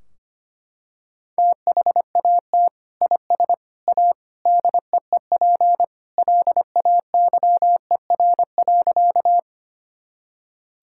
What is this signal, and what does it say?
Morse code, heard as a keyed tone, about 25 wpm, 700 Hz. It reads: THAT IS A DEEP LAYER.
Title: Sentences from Top 1600 Words 25wpm